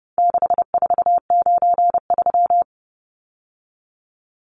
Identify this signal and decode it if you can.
Morse code, heard as a keyed tone, about 30 wpm, 700 Hz. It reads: 6493
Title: Numbers - 4-Digit 30wpm